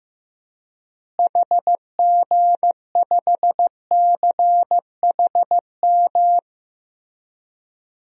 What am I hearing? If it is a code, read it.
Morse code, heard as a keyed tone, about 15 wpm, 700 Hz. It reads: HG5CHM
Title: DXCC-100 Call Signs - 6 Characters - Repeated 3x - 15wpm